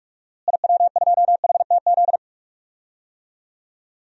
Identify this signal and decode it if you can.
Morse code, heard as a keyed tone, about 43 wpm, 700 Hz. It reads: IW2HTZ
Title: DXCC-100 Call Signs - 6 Characters - Repeated 3x - 45wpm